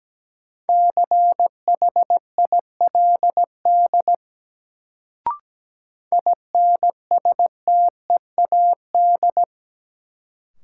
Morse code, heard as a keyed tone, about 17 wpm, 700 Hz. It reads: CHILD INSTEAD
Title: Sentences from Top 800 Words - New Word Drill - 17wpm